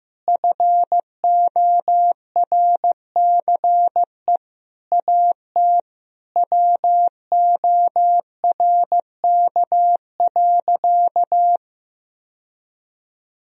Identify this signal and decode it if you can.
Morse code, heard as a keyed tone, about 15 wpm, 700 Hz. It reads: FORCE AT WORK.
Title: Sentences from Top 1000 Words - Review Part 4 - 15wpm